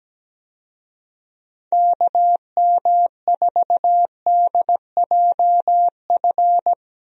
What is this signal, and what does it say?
Morse code, heard as a keyed tone, about 17 wpm, 700 Hz. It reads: KM4DJF
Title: USA Call Signs - 6 Characters - Repeated 2x - 17wpm